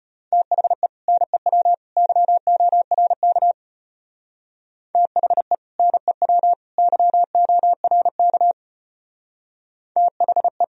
Morse code, heard as a keyed tone, about 29 wpm, 700 Hz. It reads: THE NEW YORK THE NEW YORK THE
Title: Speed-Racing - Sets of 3 Words 25wpm